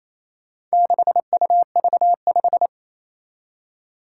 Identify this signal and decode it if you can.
Morse code, heard as a keyed tone, about 28 wpm, 700 Hz. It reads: 6UV5
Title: Four Challenging Letter-Number Combinations 28wpm